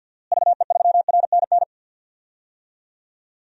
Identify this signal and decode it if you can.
Morse code, heard as a keyed tone, about 50 wpm, 700 Hz. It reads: VE3RNN
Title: DXCC-100 Call Signs - 6 Characters - Repeated 2x - 50wpm